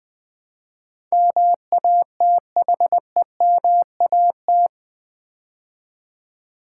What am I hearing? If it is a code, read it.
Morse code, heard as a keyed tone, about 20 wpm, 700 Hz. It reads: MATHEMAT
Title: Build A Word - 4 Letters At A Time - 11-Letter Words - 20wpm